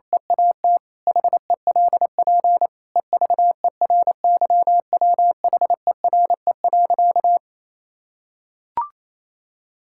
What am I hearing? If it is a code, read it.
Morse code, heard as a keyed tone, about 28 wpm, 700 Hz. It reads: EAT HELP EVERYWHERE.
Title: Sentences from Top 1600 Words 28wpm